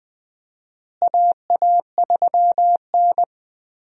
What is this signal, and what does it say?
Morse code, heard as a keyed tone, about 20 wpm, 700 Hz. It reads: AA3N
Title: USA Call Signs - 4 Characters - Repeated 2x - 20wpm